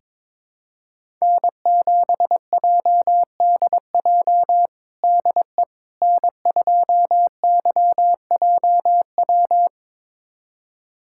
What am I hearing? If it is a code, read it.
Morse code, heard as a keyed tone, about 22 wpm, 700 Hz. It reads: N7JDJ DE N2YJW
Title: USA Callsign DE Callsign 22wpm